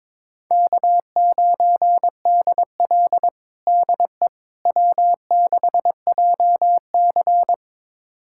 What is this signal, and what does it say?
Morse code, heard as a keyed tone, about 22 wpm, 700 Hz. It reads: K9DL DE W6JC
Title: USA Callsign DE Callsign - 4 Characters - 22wpm